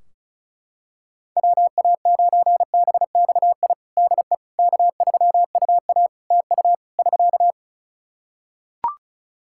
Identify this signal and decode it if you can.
Morse code, heard as a keyed tone, about 34 wpm, 700 Hz. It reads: WA9BXI DE K3UA TU <SK>
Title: K1USN SST QSO Practice 35wpm